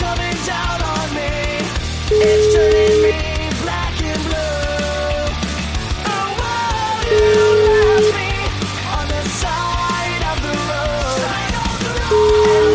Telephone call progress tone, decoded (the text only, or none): ringback tone